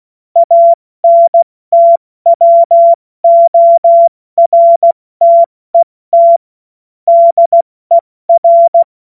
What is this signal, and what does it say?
Morse code, heard as a keyed tone, about 16 wpm, 680 Hz. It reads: ANTWORTET DER